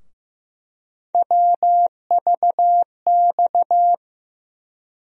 Morse code, heard as a keyed tone, about 15 wpm, 700 Hz. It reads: WVX
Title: USA Call Sign Postfixes - 3 Letters 15wpm